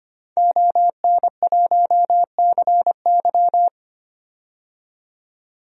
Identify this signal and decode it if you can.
Morse code, heard as a keyed tone, about 25 wpm, 700 Hz. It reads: ON1CY
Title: DXCC-100 Call Signs - 5 Characters - 25wpm